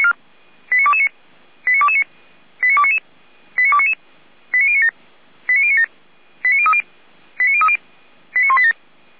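Five-tone selective call (ZVEI), heard as quick runs of five tones, tones of about 70 ms. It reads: 89229 89229 89299 89299 89097 89097 89030 89030 89187
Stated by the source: ZVEI five-tone call received by SDR, archived by feuerwehr-orlamuende.org (Orlamünde fire brigade).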